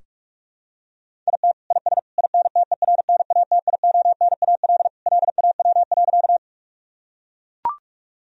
Morse code, heard as a keyed tone, about 41 wpm, 700 Hz. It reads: IT IS INTERNATIONAL LAW.
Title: Sentences from Top 1000 Words - Review Part 4 - 45wpm